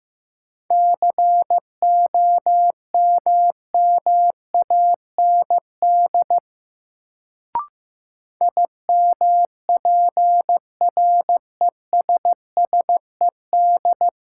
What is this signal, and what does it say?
Morse code, heard as a keyed tone, about 15 wpm, 700 Hz. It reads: COMMAND IMPRESSED
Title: Sentences from Top 1600 Words - New Word Drill Alt - 15wpm